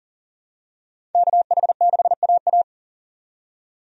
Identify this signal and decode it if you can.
Morse code, heard as a keyed tone, about 37 wpm, 700 Hz. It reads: KH6AA
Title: DXCC-100 Call Signs - 5 Characters - 40wpm